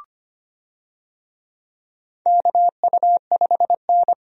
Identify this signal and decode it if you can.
Morse code, heard as a keyed tone, about 25 wpm, 700 Hz. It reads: KU5N
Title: USA Call Signs - 4 Characters - 25wpm